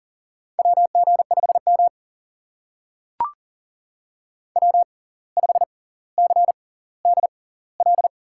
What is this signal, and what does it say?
Morse code, heard as a keyed tone, about 39 wpm, spaced as for 19 wpm, 700 Hz. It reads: WG5M W5CDL
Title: USA Call Signs - 6x Character Spacing - 40wpm